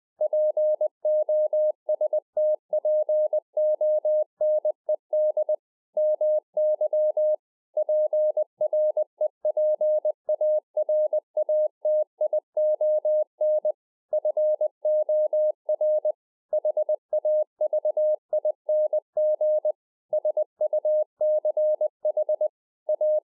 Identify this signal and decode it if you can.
Morse code, heard as a keyed tone, about 20 wpm, 610 Hz. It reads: POSTPONED MY PREPARATION FOR HAVING SUCH A
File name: 20WPM